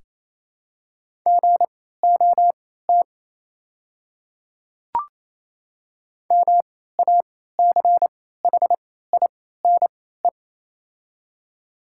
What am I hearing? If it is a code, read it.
Morse code, heard as a keyed tone, about 28 wpm, spaced as for 16 wpm, 700 Hz. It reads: GOT MACHINE
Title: 3x Character Spacing - Top 500 Words - 28wpm